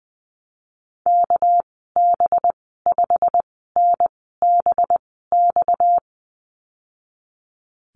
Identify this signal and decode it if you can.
Morse code, heard as a keyed tone, about 20 wpm, spaced as for 15 wpm, 700 Hz. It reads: KB5NBX
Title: 238 Lesson 38g - Letter q - Call Signs 20wpm